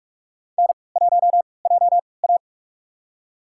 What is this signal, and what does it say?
Morse code, heard as a keyed tone, about 42 wpm, spaced as for 25 wpm, 690 Hz. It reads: N1JA
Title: USA Call Signs - 4 Characters - 3x Character Spacing - 45wpm